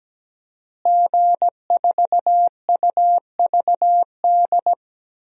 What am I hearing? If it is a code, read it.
Morse code, heard as a keyed tone, about 17 wpm, 700 Hz. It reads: G4UVD
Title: DXCC-100 Call Signs - 5 Characters - Repeated 2x - 17wpm